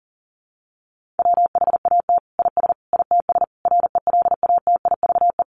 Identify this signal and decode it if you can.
Morse code, heard as a keyed tone, about 40 wpm, 700 Hz. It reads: WHAT IS ITS RELATIVE
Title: Sentences from Top 1400 Words 40wpm